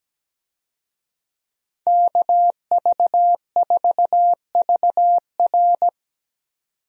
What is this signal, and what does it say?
Morse code, heard as a keyed tone, about 17 wpm, 700 Hz. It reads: KV4VR
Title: USA Call Signs - 5 Characters - 17wpm